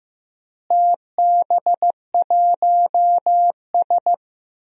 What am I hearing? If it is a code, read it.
Morse code, heard as a keyed tone, about 15 wpm, 700 Hz. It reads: TB1S
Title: DXCC-100 Call Signs - 4 Characters - Repeated 3x - 15wpm